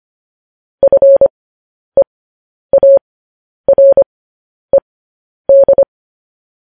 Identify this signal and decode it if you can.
Morse code, heard as a keyed tone, about 25 wpm, spaced as for 10 wpm, 550 Hz. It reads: FEARED
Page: news-headlines-in-morse-code-at-10-wpm